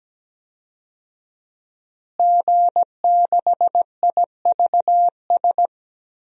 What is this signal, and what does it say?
Morse code, heard as a keyed tone, about 17 wpm, 700 Hz. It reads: G6IVS
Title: DXCC-100 Call Signs - 5 Characters - Repeated 3x - 17wpm